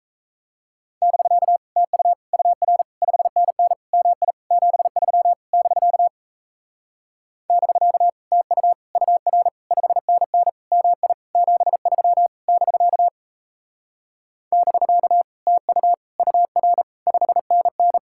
Morse code, heard as a keyed tone, about 36 wpm, 700 Hz. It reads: <BK> TU UR 5NN MI 73 <BK> <BK> TU UR 5NN MI 73 <BK> <BK> TU UR 5NN